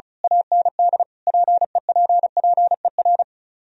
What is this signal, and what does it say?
Morse code, heard as a keyed tone, about 33 wpm, 700 Hz. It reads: AND PEPPER
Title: Speed-Racing - Sets of 5 Words 35wpm